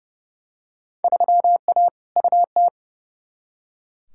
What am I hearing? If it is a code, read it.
Morse code, heard as a keyed tone, about 30 wpm, 700 Hz. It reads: 3A UT
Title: Field Day Exchanges - 30wpm